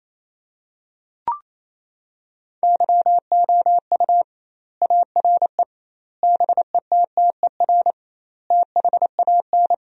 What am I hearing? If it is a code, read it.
Morse code, heard as a keyed tone, about 28 wpm, 700 Hz. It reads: YOU ARE BETTER THAN
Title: 2x Word-Spacing - Sentences from Top 400 Words - 28wpm